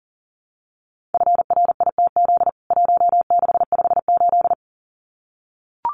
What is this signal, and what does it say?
Morse code, heard as a keyed tone, about 40 wpm, 700 Hz. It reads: FRITZ 1658
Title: CWT Exchange 40wpm